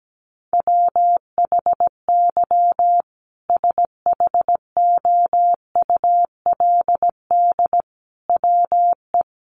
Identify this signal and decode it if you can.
Morse code, heard as a keyed tone, about 17 wpm, 700 Hz. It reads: WHY SHOULD WE
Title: Sentences from Top 1500 Words 17wpm